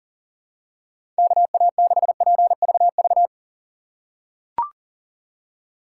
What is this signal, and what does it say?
Morse code, heard as a keyed tone, about 40 wpm, 700 Hz. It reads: KA6PVV
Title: USA Call Signs - 3x Character Spacing - 40wpm